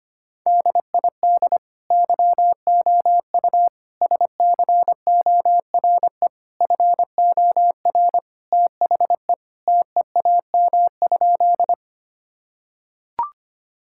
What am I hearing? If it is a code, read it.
Morse code, heard as a keyed tone, about 25 wpm, 700 Hz. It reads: DID YOU SCORE FOR THE TEAM?
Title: Sentences from Top 1600 Words 25wpm